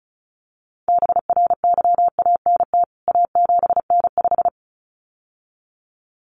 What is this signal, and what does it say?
Morse code, heard as a keyed tone, about 34 wpm, 700 Hz. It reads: BRYANT A7N5
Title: CWT Exchange 35wpm